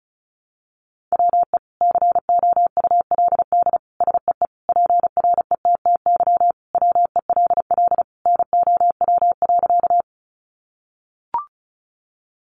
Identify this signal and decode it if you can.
Morse code, heard as a keyed tone, about 34 wpm, 700 Hz. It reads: WE COULD SEE PRETTY WELL NOW.
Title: Sentences from Top 1000 Words 35wpm